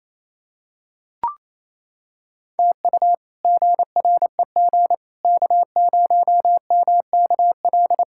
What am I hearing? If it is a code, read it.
Morse code, heard as a keyed tone, about 28 wpm, 700 Hz. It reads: TU GREG K0MKL
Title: K1USN SST QSO Practice - 28wpm